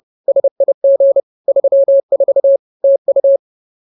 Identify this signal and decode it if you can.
Morse code, heard as a keyed tone, about 30 wpm, 550 Hz. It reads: SIG 34 TU